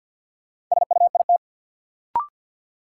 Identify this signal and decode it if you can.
Morse code, heard as a keyed tone, about 52 wpm, 700 Hz. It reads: SUIT
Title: Sentences from Top 1600 Words - New Word Drill Alt - 50wpm